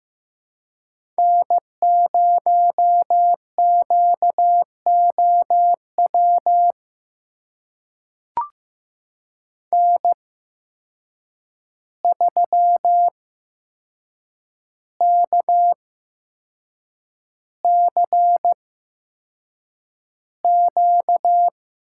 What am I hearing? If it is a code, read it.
Morse code, heard as a keyed tone, about 15 wpm, spaced as for 6 wpm, 700 Hz. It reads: N0QOW N3KCQ
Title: USA Call Signs - 5 Characters - 8x Character Spacing - 15wpm